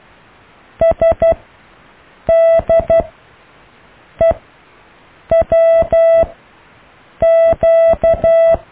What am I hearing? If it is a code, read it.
Morse code, heard as a keyed tone, about 12 wpm, spaced as for 6 wpm, 660 Hz. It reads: SDEWQ
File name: cw / Alle letters